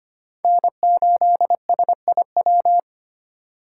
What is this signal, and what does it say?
Morse code, heard as a keyed tone, about 25 wpm, 700 Hz. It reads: N8SIW
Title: USA Call Signs - 5 Characters - 4x Character Spacing - 25wpm